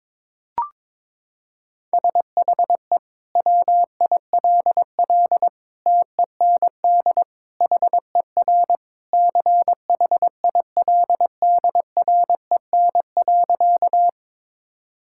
Morse code, heard as a keyed tone, about 21 wpm, 700 Hz. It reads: SHE WILL TEND HER CHILDREN.
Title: Sentences from Top 1000 Words - Review Part 5 - 22wpm